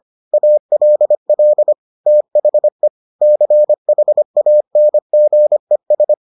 Morse code, heard as a keyed tone, about 25 wpm, 600 Hz. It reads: ALL THE CHANGES